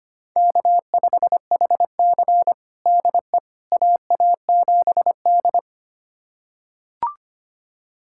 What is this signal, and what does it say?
Morse code, heard as a keyed tone, about 25 wpm, 700 Hz. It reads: K5HC DE AA7D
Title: USA Callsign DE Callsign - 4 Characters - 25wpm